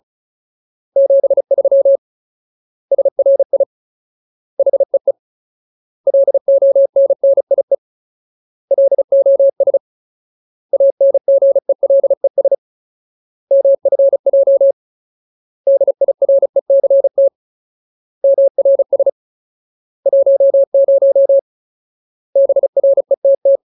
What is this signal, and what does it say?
Morse code, heard as a keyed tone, about 34 wpm, 550 Hz. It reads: 73 SRI HEE LONNIE LOS ANGELES MFJ DIRECT MRS 10 BRETT
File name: Ultimate Word List / 35WPM Ultimate